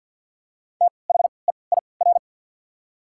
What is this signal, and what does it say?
Morse code, heard as a keyed tone, about 48 wpm, spaced as for 28 wpm, 690 Hz. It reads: THEIR
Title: 3x Character Spacing - Top 300 Words - 50wpm